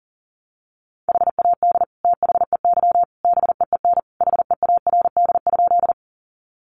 Morse code, heard as a keyed tone, about 40 wpm, 700 Hz. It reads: HAD THEY BEEN HEARD?
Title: Sentences from Top 400 Words 40wpm